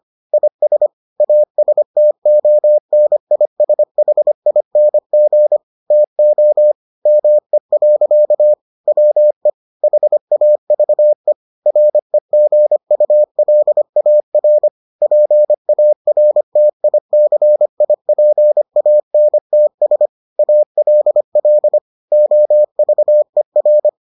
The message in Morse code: IS ASTONISHING TO ME. WE HAVE REGULAR PARTICIPANTS ALL OVER

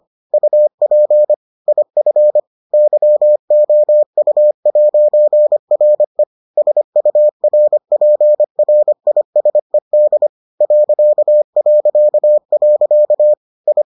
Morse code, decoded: UP IF YOU'RE SURPRISED ... I

